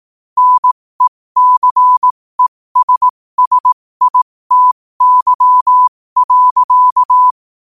Morse code, heard as a keyed tone, about 18 wpm, spaced as for 15 wpm, 1000 Hz. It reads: NECESSITY.
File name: qotdfiles / 15wpm